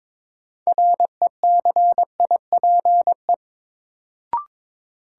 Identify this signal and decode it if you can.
Morse code, heard as a keyed tone, about 22 wpm, 700 Hz. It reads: RECIPE
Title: Sentences from Top 1200 Words - New Word Drill - 22wpm